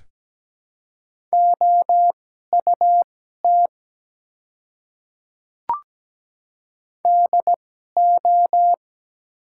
Morse code, heard as a keyed tone, about 17 wpm, spaced as for 12 wpm, 700 Hz. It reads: OUT DO
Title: 2x Character Spacing - Top 100 Words - 17wpm